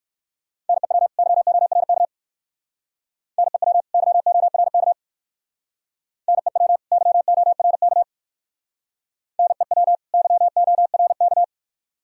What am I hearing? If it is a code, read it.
Morse code, heard as a keyed tone, about 53 wpm, 700 Hz. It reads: NEW YORK NEW YORK NEW YORK NEW YORK
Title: Speed-Racing - Sets of 2 Words 45wpm